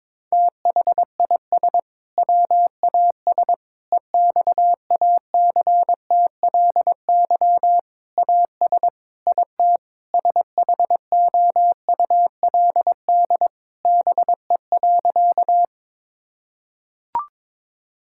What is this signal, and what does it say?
Morse code, heard as a keyed tone, about 22 wpm, 700 Hz. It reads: THIS WAS EXACTLY AS IT SHOULD BE.
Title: Sentences from Top 1000 Words - Review Part 6 - 22wpm